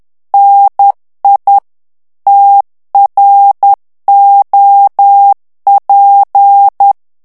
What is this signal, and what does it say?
Morse code, heard as a keyed tone, about 10 wpm, 800 Hz. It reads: NI TROP
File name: FRANCAIS TEXTE CLAIR 2 VITESSE 10 MOTS MINUTES